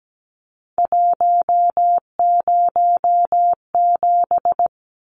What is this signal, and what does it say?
Morse code, heard as a keyed tone, about 17 wpm, 700 Hz. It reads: 107
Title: Numbers - 3-Digit 17wpm